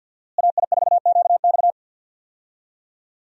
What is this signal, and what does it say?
Morse code, heard as a keyed tone, about 48 wpm, 700 Hz. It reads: AI4QX
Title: USA Call Signs - 5 Characters - Repeated 2x - 50wpm